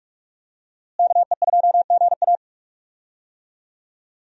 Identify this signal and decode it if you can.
Morse code, heard as a keyed tone, about 41 wpm, 700 Hz. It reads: KE2GA
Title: USA Call Signs - 5 Characters - Repeated 3x - 45wpm